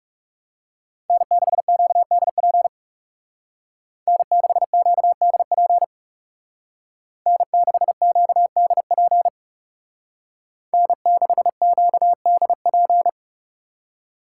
Morse code, heard as a keyed tone, about 39 wpm, 700 Hz. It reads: N6QDP N6QDP N6QDP N6QDP